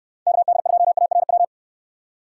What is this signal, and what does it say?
Morse code, heard as a keyed tone, about 66 wpm, 700 Hz. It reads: KD2ARW